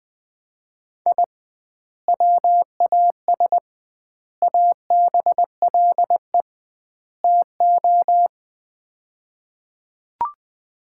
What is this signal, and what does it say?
Morse code, heard as a keyed tone, about 20 wpm, 700 Hz. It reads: I WAS ABLE TO
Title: 2x Word-Spacing - Sets of 4 Words - 20wpm